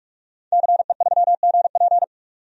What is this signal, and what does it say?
Morse code, heard as a keyed tone, about 46 wpm, 700 Hz. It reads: CE3GP
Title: DXCC-100 Call Signs - 5 Characters - 45wpm